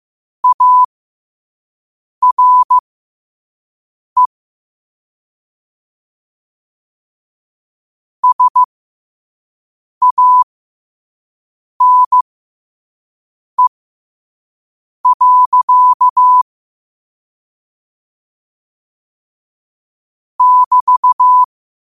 Morse code, heard as a keyed tone, about 15 wpm, spaced as for 5 wpm, 1000 Hz. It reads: ARE SANE. <BT>
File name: qotdfiles / 5wpm